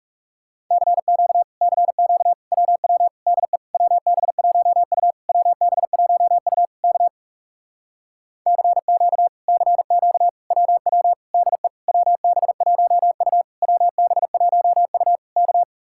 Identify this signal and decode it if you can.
Morse code, heard as a keyed tone, about 38 wpm, 700 Hz. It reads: CQ CQ WW DE WB1U WB1U K CQ CQ WW DE WB1U WB1U K
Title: Speed-Racing Common QSO Phrases - 30wpm